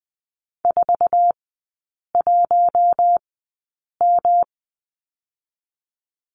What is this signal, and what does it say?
Morse code, heard as a keyed tone, about 20 wpm, spaced as for 8 wpm, 700 Hz. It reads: 41M
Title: 133 Lesson 22e - Question Mark - Three Characters 20wpm